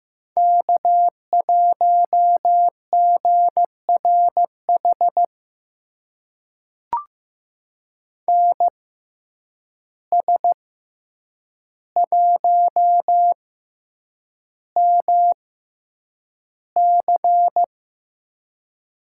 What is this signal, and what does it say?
Morse code, heard as a keyed tone, about 15 wpm, spaced as for 8 wpm, 700 Hz. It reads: K1GRH NS1MC
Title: USA Call Signs - 5 Characters - 6x Character Spacing - 15wpm